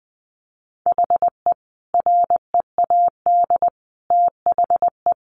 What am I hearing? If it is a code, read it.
Morse code, heard as a keyed tone, about 20 wpm, 700 Hz. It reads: HE READ THE
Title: Sentences from Top 600 Words 20wpm